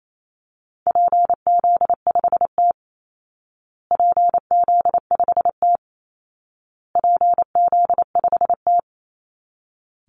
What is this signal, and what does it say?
Morse code, heard as a keyed tone, about 28 wpm, 700 Hz. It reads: PZ5T PZ5T PZ5T
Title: Call Signs Repeated 2x - 28wpm